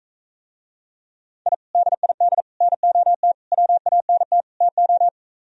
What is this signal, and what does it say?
Morse code, heard as a keyed tone, about 45 wpm, 700 Hz. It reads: I DID NOT WANT TO